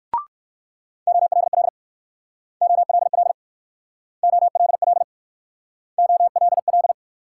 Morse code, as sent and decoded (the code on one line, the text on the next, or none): --- .-.. .-.. / --- .-.. .-.. / --- .-.. .-.. / --- .-.. .-..
OLL OLL OLL OLL